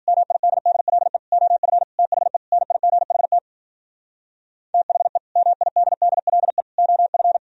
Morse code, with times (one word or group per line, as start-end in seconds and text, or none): 0.1-1.2 MIDDLE
1.3-1.8 OF
2.0-2.4 THE
2.5-3.4 NIGHT
4.7-5.2 THE
5.4-6.6 MIDDLE
6.8-7.4 OF